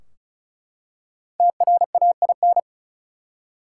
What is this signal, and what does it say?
Morse code, heard as a keyed tone, about 34 wpm, 700 Hz. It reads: TRAIN
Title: Five Letter Words - 35wpm